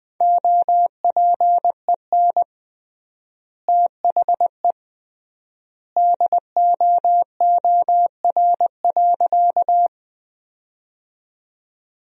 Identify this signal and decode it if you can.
Morse code, heard as a keyed tone, about 20 wpm, 700 Hz. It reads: OPEN THE DOOR.
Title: 3x Word-Spacing - Sentences from Top 300 Words - 20wpm